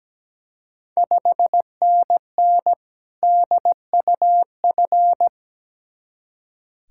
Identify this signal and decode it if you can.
Morse code, heard as a keyed tone, about 17 wpm, 700 Hz. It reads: 5NN DUF
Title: Ontario QSO Party Exchange 17wpm